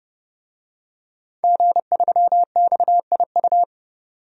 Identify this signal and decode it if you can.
Morse code, heard as a keyed tone, about 30 wpm, 700 Hz. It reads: G3XIU
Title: DXCC-100 Call Signs - 5 Characters - Repeated 2x - 30wpm